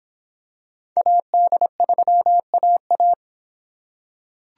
Morse code, heard as a keyed tone, about 26 wpm, 700 Hz. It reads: AD3AA